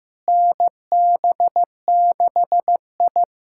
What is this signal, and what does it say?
Morse code, heard as a keyed tone, about 15 wpm, 700 Hz. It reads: NB6I
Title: USA Callsign DE Callsign - 4 Characters - 15wpm